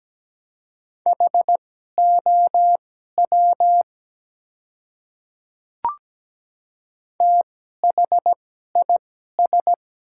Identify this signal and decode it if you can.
Morse code, heard as a keyed tone, about 17 wpm, spaced as for 12 wpm, 700 Hz. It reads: HOW THIS
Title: 2x Character Spacing - Top 300 Words - 17wpm